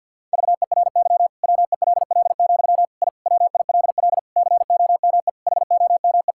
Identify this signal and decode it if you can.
Morse code, heard as a keyed tone, about 52 wpm, 700 Hz. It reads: VERY WELL, I WILL COME HOME
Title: Sentences from Top 1000 Words - Review Part 5 - 50wpm